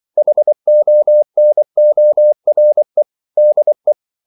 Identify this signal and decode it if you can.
Morse code, heard as a keyed tone, about 24 wpm, 600 Hz. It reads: HONORE DE
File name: CW / Practice / EnglishQuotations0324